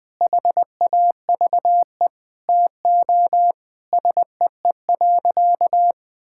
Morse code, heard as a keyed tone, about 19 wpm, 700 Hz. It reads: HAVE TO SEE.
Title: Sentences with 3-Word Phrases - 20wpm